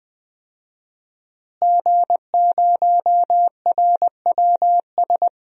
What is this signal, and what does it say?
Morse code, heard as a keyed tone, about 20 wpm, 700 Hz. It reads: G0RWS
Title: DXCC-100 Call Signs - 5 Characters - Repeated 2x - 20wpm